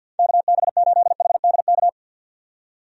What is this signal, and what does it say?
Morse code, heard as a keyed tone, about 52 wpm, 700 Hz. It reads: KB8HDK
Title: USA Call Signs - 6 Characters - 50wpm